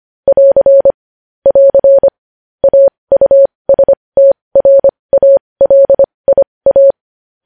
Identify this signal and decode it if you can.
Morse code, heard as a keyed tone, about 25 wpm, spaced as for 20 wpm, 550 Hz. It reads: <AR> <AR> AUSTRALIA
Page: News Headlines in Morse Code at 20 WPM